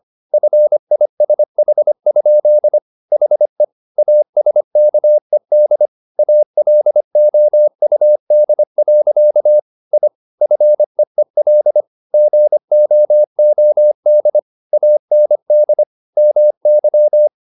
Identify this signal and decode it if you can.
Morse code, heard as a keyed tone, about 25 wpm, 600 Hz. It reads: FISH? HE ASKED ALOUD. I FEEL GOOD AND MY